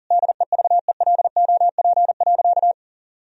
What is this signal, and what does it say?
Morse code, heard as a keyed tone, about 40 wpm, 700 Hz. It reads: DEVELOP.
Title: Sentences from Top 1000 Words - Review Part 2 - 40wpm